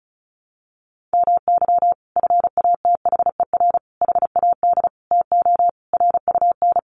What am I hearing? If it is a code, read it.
Morse code, heard as a keyed tone, about 34 wpm, 700 Hz. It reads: MY FATHER HAD TO RUN
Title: Sentences from Top 1100 Words 35wpm